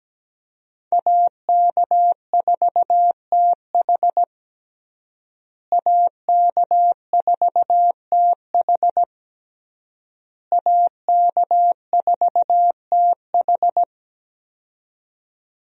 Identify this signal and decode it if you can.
Morse code, heard as a keyed tone, about 17 wpm, 700 Hz. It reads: AK4TH AK4TH AK4TH
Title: USA Call Signs - 5 Characters - Repeated 3x - 17wpm